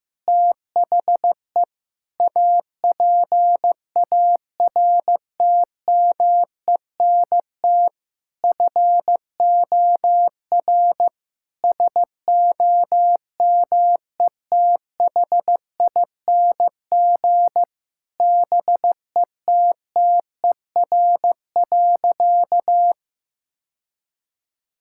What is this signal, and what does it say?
Morse code, heard as a keyed tone, about 15 wpm, 700 Hz. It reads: THE APARTMENT FOR SOMETHING BETTER.
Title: Sentences from Top 1000 Words - Review Part 6 - 15wpm